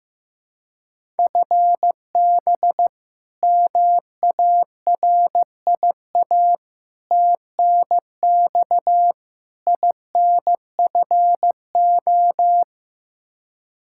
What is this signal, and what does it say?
Morse code, heard as a keyed tone, about 15 wpm, 700 Hz. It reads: FB MARIA TNX INFO
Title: Practice QSO - Exchange 3 - More Info - 15wpm